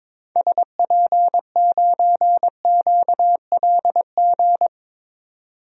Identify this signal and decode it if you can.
Morse code, heard as a keyed tone, about 22 wpm, 700 Hz. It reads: SP9QLG
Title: DXCC-100 Call Signs - 6 Characters - 22wpm